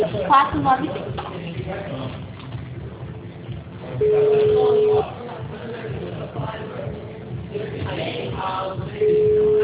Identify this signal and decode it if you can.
Ringback tone (Europe, 425 Hz), with speech.